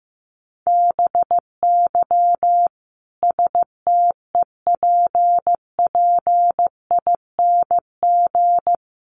Morse code, heard as a keyed tone, about 15 wpm, 700 Hz. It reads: BY STEPPING
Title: A Sound Of Thunder 15wpm